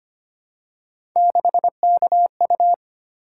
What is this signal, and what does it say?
Morse code, heard as a keyed tone, about 25 wpm, 700 Hz. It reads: 6KU
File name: Three Letter-Number Combinations - 25wpm